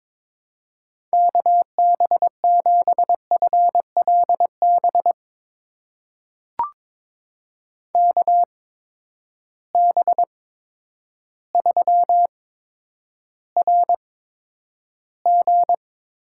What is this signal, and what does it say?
Morse code, heard as a keyed tone, about 22 wpm, spaced as for 10 wpm, 700 Hz. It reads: KB7FLB KB3RG
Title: USA Call Signs - 6 Characters - 8x Character Spacing - 22wpm